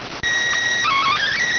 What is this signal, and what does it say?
SSTV (slow-scan television), PD 290.